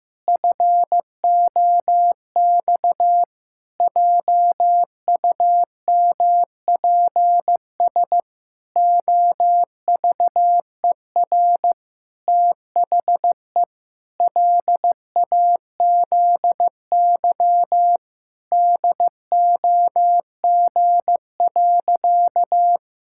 Morse code, heard as a keyed tone, about 16 wpm, 700 Hz. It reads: FOX JUMPS OVER THE LAZY DOG.
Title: Pangram Challenge v2 - 15wpm